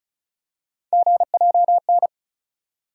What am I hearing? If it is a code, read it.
Morse code, heard as a keyed tone, about 34 wpm, 700 Hz. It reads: GJN
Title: USA Call Sign Postfixes - 3 Letters 35wpm